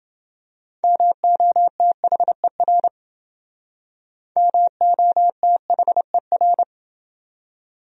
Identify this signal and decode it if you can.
Morse code, heard as a keyed tone, about 29 wpm, 700 Hz. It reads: MOTHER MOTHER